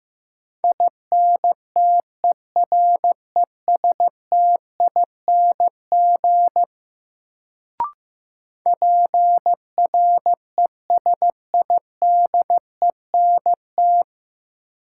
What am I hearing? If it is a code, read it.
Morse code, heard as a keyed tone, about 15 wpm, 700 Hz. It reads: INTERESTING PRESIDENT
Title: Sentences from Top 1000 Words - New Word Drill - 15wpm